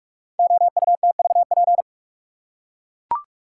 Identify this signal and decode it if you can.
Morse code, heard as a keyed tone, about 44 wpm, 690 Hz. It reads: OUTVP
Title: Code Groups - 5 Letter:Numbers - 45wpm